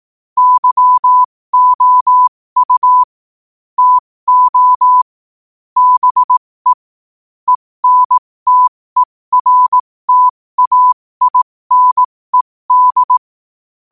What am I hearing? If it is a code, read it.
Morse code, heard as a keyed tone, about 18 wpm, spaced as for 15 wpm, 1000 Hz. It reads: YOU TO BE ENTERTAINED
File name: qotdfiles / 15wpm